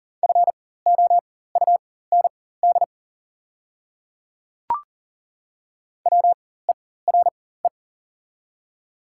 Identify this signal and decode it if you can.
Morse code, heard as a keyed tone, about 38 wpm, spaced as for 18 wpm, 690 Hz. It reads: FOUND WERE
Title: 4x Character Spacing - Top 300 Words - 40wpm